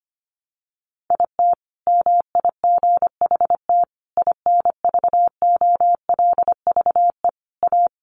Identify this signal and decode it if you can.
Morse code, heard as a keyed tone, about 25 wpm, 700 Hz. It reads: IT MIGHT INVOLVE A
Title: Sentences from Top 1200 Words 25wpm